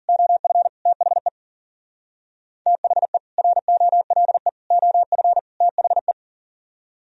Morse code, heard as a keyed tone, about 51 wpm, 700 Hz. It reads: OF THE THE ROLE OF THE